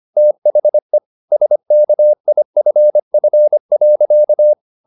Morse code, heard as a keyed tone, about 25 wpm, 600 Hz. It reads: THE SKIFF.